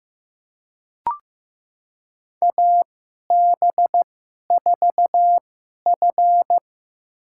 Morse code, heard as a keyed tone, about 15 wpm, spaced as for 11 wpm, 700 Hz. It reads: AB4F
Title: USA Call Signs - 4 Characters - 2x Character Spacing - 15wpm